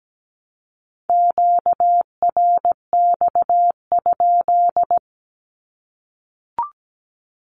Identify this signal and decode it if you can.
Morse code, heard as a keyed tone, about 17 wpm, 700 Hz. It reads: QRX?
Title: Q-Codes 17wpm